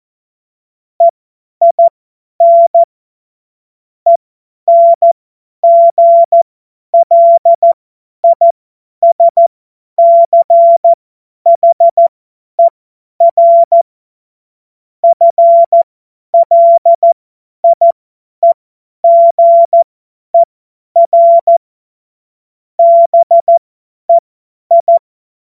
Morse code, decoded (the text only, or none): EIN ENGLISCHER FLIEGER BEI